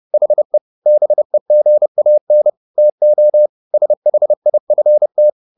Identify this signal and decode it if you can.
Morse code, heard as a keyed tone, about 30 wpm, 600 Hz. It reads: HE BEGAN TO SHIFT
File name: the old man and the sea / the old man and the sea 1 30wpm